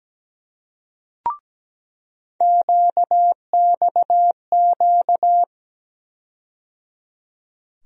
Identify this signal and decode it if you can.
Morse code, heard as a keyed tone, about 17 wpm, 700 Hz. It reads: QXQ
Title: USA Call Sign Postfixes - 3 Letters 17wpm